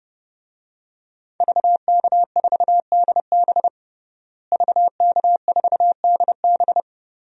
Morse code, heard as a keyed tone, about 30 wpm, 700 Hz. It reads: VK4DB VK4DB
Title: DXCC-100 Call Signs - 5 Characters - Repeated 3x - 30wpm